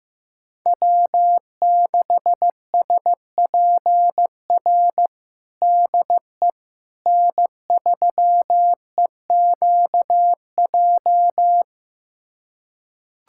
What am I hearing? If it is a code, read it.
Morse code, heard as a keyed tone, about 15 wpm, 700 Hz. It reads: W6SPR DE N3EQJ
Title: USA Callsign DE Callsign - 5 Characters - 15wpm